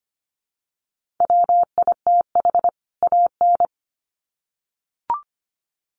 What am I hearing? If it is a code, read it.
Morse code, heard as a keyed tone, about 25 wpm, 700 Hz. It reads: WITH AN
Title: Sets of 2 Words 25wpm